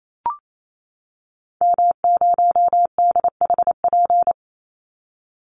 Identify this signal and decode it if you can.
Morse code, heard as a keyed tone, about 28 wpm, 700 Hz. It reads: M0DHP
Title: Call Signs - 28wpm